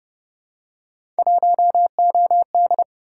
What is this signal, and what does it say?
Morse code, heard as a keyed tone, about 29 wpm, 700 Hz. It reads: 1OD